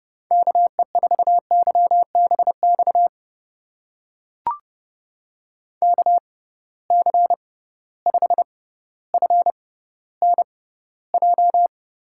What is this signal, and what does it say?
Morse code, heard as a keyed tone, about 31 wpm, spaced as for 16 wpm, 700 Hz. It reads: KE4YBX KC5FNJ
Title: USA Call Signs - 6 Characters - 6x Character Spacing - 30wpm